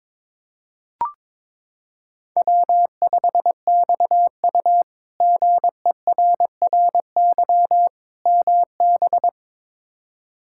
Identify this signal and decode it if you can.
Morse code, heard as a keyed tone, about 22 wpm, 700 Hz. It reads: W5XU GERRY MB
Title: K1USN SST QSO Practice - 22wpm